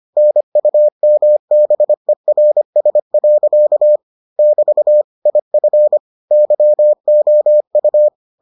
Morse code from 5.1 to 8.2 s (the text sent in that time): IF YOU